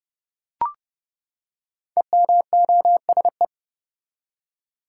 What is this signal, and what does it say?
Morse code, heard as a keyed tone, about 30 wpm, 700 Hz. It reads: EMOSE
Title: Code Groups - 5 Letters - 30wpm